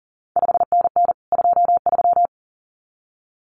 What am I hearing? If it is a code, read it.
Morse code, heard as a keyed tone, about 38 wpm, 700 Hz. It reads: 5NN 23
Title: CQ WW DX Contest Exchange Encore - 40wpm